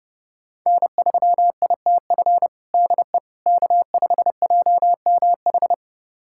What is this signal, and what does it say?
Morse code, heard as a keyed tone, about 30 wpm, 700 Hz. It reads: N3ITF DE K5JMH
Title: USA Callsign DE Callsign - 5 Characters - 30wpm